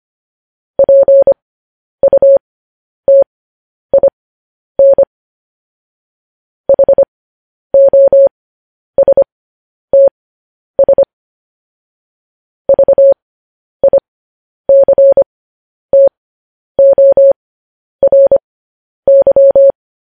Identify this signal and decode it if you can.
Morse code, heard as a keyed tone, about 25 wpm, spaced as for 10 wpm, 550 Hz. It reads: PUTIN HOSTS VICTORY
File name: podcast / News headlines-1x-25x10